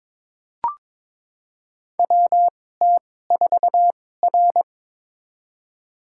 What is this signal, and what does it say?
Morse code, heard as a keyed tone, about 22 wpm, spaced as for 16 wpm, 700 Hz. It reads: WT4R
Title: USA Call Signs - 4 Characters - 2x Character Spacing - 22wpm